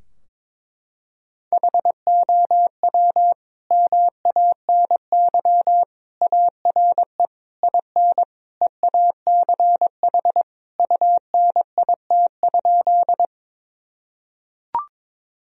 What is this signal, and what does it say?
Morse code, heard as a keyed tone, about 22 wpm, 700 Hz. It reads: HOW MANY ARE IN EACH UNIT?
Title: Sentences from Top 1000 Words - Review Part 2 - 22wpm